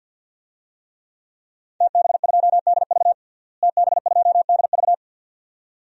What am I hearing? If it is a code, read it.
Morse code, heard as a keyed tone, about 48 wpm, 700 Hz. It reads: TB2DV TB2DV
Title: DXCC-100 Call Signs - 5 Characters - Repeated 2x - 50wpm